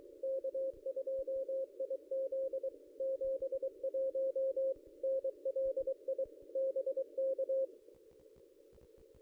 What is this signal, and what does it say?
Morse code, heard as a keyed tone, about 23 wpm, 530 Hz. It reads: K2IZ 71 NLI BK
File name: contests / ss-cw-k5tr-op-n5rz -1800